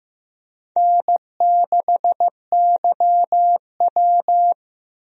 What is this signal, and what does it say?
Morse code, heard as a keyed tone, about 15 wpm, 700 Hz. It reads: N6YW
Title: USA Call Signs - 4 Characters - Repeated 2x - 15wpm